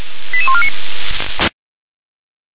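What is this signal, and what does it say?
Five-tone selective call (ZVEI), heard as quick runs of five tones, tones of about 70 ms. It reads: 88149
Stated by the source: ZVEI five-tone call received on SDR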